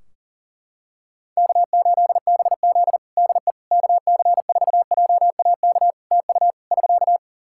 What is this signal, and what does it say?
Morse code, heard as a keyed tone, about 40 wpm, 700 Hz. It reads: K8BZ DE KC4JAK TU <SK>
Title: Practice QSO - Exchange 4 - Ending - 40wpm